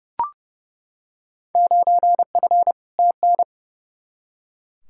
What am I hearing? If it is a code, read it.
Morse code, heard as a keyed tone, about 30 wpm, 700 Hz. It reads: 9F TN
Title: Field Day Exchanges Encore - 30wpm